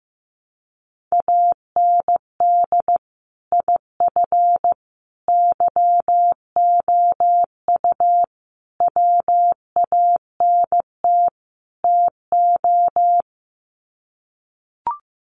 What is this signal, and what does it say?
Morse code, heard as a keyed tone, about 15 wpm, 700 Hz. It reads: AND IF YOU WANT TO
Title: Sets of 5 Words Encore 15wpm